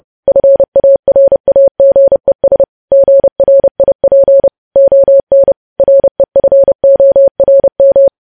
Morse code, heard as a keyed tone, about 30 wpm, 550 Hz. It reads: FARAGES GRIP ON REFORM
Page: news headlines in morse code at 30 wpm